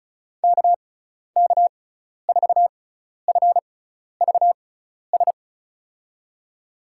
Morse code, heard as a keyed tone, about 34 wpm, spaced as for 12 wpm, 700 Hz. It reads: KK4FVS
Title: USA Call Signs - 6 Characters - 6x Character Spacing - 35wpm